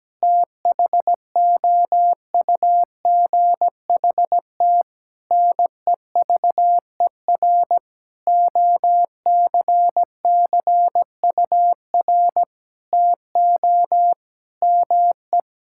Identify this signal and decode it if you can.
Morse code, heard as a keyed tone, about 17 wpm, 700 Hz. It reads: THOUGHT NEVER OCCUR TO ME
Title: Sentences from Top 1000 Words - Review Part 6 - 17wpm